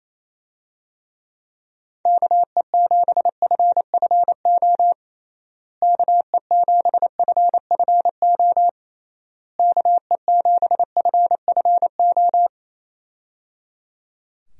Morse code, heard as a keyed tone, about 28 wpm, 700 Hz. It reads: KE7FFO KE7FFO KE7FFO
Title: USA Call Signs - 6 Characters - Repeated 3x - 28wpm